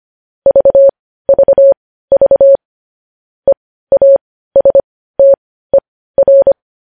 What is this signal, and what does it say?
Morse code, heard as a keyed tone, about 25 wpm, spaced as for 15 wpm, 550 Hz. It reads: VVV EASTER
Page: news-headlines-in-morse-code-at-15-wpm